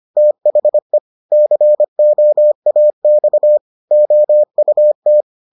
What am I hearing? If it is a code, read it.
Morse code, heard as a keyed tone, about 25 wpm, 600 Hz. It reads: THE COAX OUT